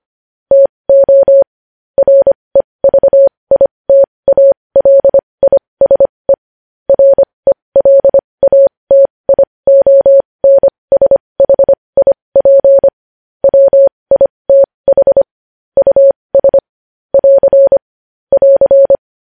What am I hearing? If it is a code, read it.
Morse code, heard as a keyed tone, about 25 wpm, spaced as for 20 wpm, 550 Hz. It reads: TO REVITALISE RELATIONSHIP WITH US <AR> <AR>
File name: podcast / News headlines-1x-25x20